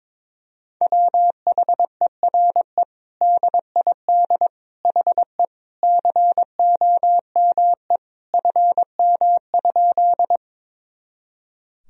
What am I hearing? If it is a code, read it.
Morse code, heard as a keyed tone, about 22 wpm, 700 Hz. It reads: WHERE DID HE COME FM?
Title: Sentences from Top 200 Words and Abbreviations 22wpm